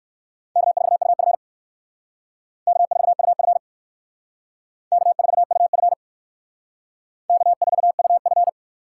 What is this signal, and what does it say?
Morse code, heard as a keyed tone, about 58 wpm, 700 Hz. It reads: K4UF K4UF K4UF K4UF